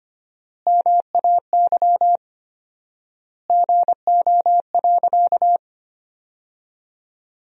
Morse code, heard as a keyed tone, about 25 wpm, 700 Hz. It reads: MAY GO.
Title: 4x Word-Spacing - Sentences from Top 100 Words - 25wpm